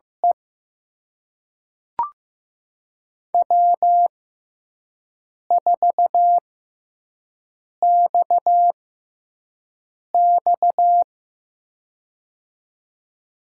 Morse code, heard as a keyed tone, about 15 wpm, spaced as for 5 wpm, 700 Hz. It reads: E W4XX
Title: USA Call Signs - 4 Characters - 6x Character Spacing - 15wpm